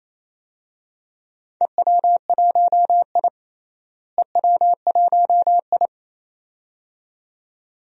Morse code, heard as a keyed tone, about 28 wpm, 700 Hz. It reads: EW1I EW1I
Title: DXCC-100 Call Signs - 4 Characters - Repeated 2x - 28wpm